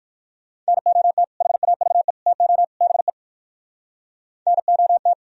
Morse code, heard as a keyed tone, about 57 wpm, 700 Hz. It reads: NOT HAVE TO BE NOT